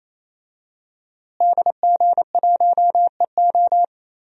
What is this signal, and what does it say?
Morse code, heard as a keyed tone, about 28 wpm, 700 Hz. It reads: DG1EO
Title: DXCC-100 Call Signs - 5 Characters - Repeated 3x - 28wpm